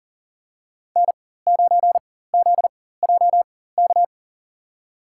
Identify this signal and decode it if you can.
Morse code, heard as a keyed tone, about 40 wpm, spaced as for 19 wpm, 690 Hz. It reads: N9ZJK
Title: USA Call Signs - 4x Character Spacing - 40wpm